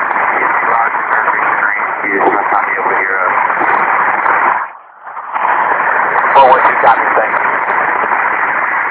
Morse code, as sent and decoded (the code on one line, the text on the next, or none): none